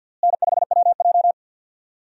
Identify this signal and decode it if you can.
Morse code, heard as a keyed tone, about 52 wpm, 700 Hz. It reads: N5WJ
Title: USA Call Signs - 4 Characters - Repeated 2x - 50wpm